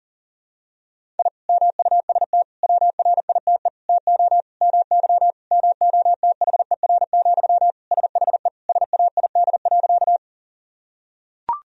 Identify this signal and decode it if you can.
Morse code, heard as a keyed tone, about 40 wpm, 700 Hz. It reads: I MUST WRITE TO MY MOTHER, SHE SAID.
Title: Sentences from Top 1000 Words - Review Part 1 - 40wpm